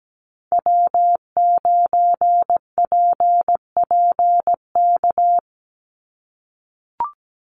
Morse code, heard as a keyed tone, about 17 wpm, 700 Hz. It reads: W9PPK
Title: USA Call Signs 17wpm